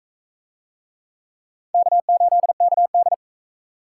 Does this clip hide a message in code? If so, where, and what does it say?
Yes: Morse code from 1.7 to 3.1 s, heard as a keyed tone, about 42 wpm, 700 Hz. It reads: K8KD